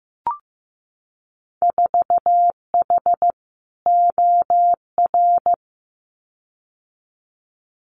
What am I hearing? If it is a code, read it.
Morse code, heard as a keyed tone, about 15 wpm, 700 Hz. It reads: 4H OR
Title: Winter Field Day Exchanges Encore 15wpm